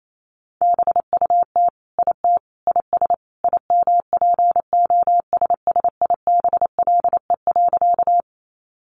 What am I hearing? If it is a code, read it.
Morse code, heard as a keyed tone, about 28 wpm, 700 Hz. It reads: BUT IT IS IMPOSSIBLE.
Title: Sentences from Top 1100 Words 28wpm